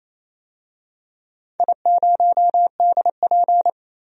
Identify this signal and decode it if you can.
Morse code, heard as a keyed tone, about 28 wpm, 700 Hz. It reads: I0DP
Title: DXCC-100 Call Signs - 4 Characters - 28wpm